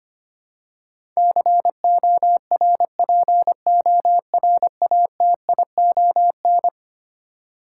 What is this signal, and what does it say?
Morse code, heard as a keyed tone, about 25 wpm, 700 Hz. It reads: CORPORATION
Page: morse-code-ninja-practice-25wpm